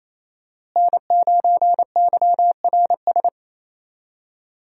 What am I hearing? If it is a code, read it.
Morse code, heard as a keyed tone, about 27 wpm, 700 Hz. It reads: N9YRS
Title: USA Call Signs - 5 Characters - Repeated 3x - 28wpm